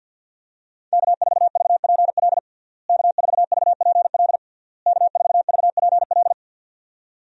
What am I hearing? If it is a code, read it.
Morse code, heard as a keyed tone, about 52 wpm, 700 Hz. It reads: K4VPL K4VPL K4VPL
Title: USA Call Signs - 5 Characters - Repeated 3x - 50wpm